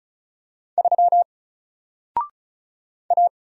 Morse code, heard as a keyed tone, about 36 wpm, 700 Hz. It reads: 3 A
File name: Single-Letter-Number-35wpm